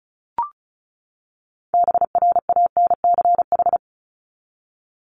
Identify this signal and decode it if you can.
Morse code, heard as a keyed tone, about 34 wpm, 700 Hz. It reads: BRANCH
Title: Top 1751-2000 Words 35wpm